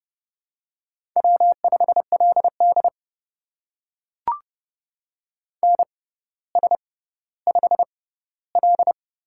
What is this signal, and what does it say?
Morse code, heard as a keyed tone, about 30 wpm, spaced as for 15 wpm, 700 Hz. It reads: W5LD NS5L
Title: USA Call Signs - 4 Characters - 6x Character Spacing - 30wpm